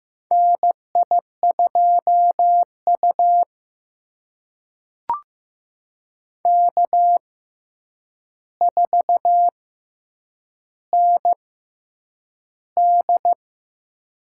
Morse code, heard as a keyed tone, about 15 wpm, spaced as for 8 wpm, 700 Hz. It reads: NI2U K4ND
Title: USA Call Signs - 4 Characters - 6x Character Spacing - 15wpm